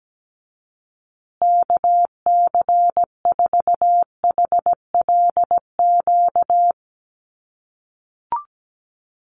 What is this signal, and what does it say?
Morse code, heard as a keyed tone, about 17 wpm, 700 Hz. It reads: KC4HLQ
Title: USA Call Signs Repeated 3x 17wpm